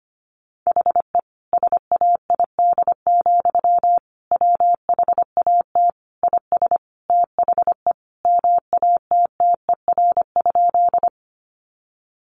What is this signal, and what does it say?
Morse code, heard as a keyed tone, about 25 wpm, 700 Hz. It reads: HE SAID, WHAT IS THE MATTER?
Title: Sentences from Top 700 Words 25wpm